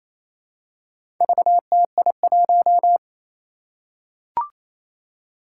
Morse code, heard as a keyed tone, about 28 wpm, 700 Hz. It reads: VTI1
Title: Four Challenging Letter-Number Combinations 28wpm